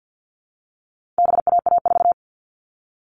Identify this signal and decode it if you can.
Morse code, heard as a keyed tone, about 50 wpm, 700 Hz. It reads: BAA4
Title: All Cut Numbers - 4-Digit 50wpm